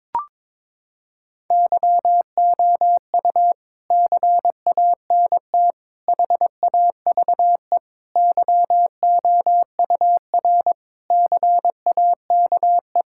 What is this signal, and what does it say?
Morse code, heard as a keyed tone, about 22 wpm, 700 Hz. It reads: YOU CANT HAVE YOUR CAKE
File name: Sets of 9 Words - 22wpm